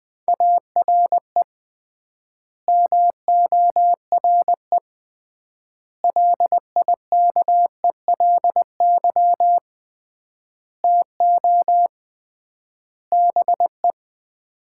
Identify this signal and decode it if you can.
Morse code, heard as a keyed tone, about 21 wpm, 700 Hz. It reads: ARE MORE LIKELY TO BE
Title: 3x Word-Spacing - Sets of 5 Words - 20wpm